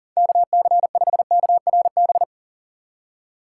Morse code, heard as a keyed tone, about 40 wpm, 700 Hz. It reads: KC5KRB
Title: USA Call Signs - 3x Character Spacing - 40wpm